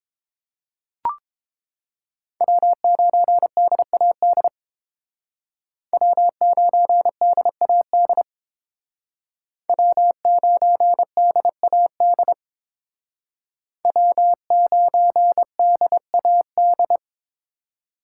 Morse code, heard as a keyed tone, about 28 wpm, 700 Hz. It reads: W9DAD W9DAD W9DAD W9DAD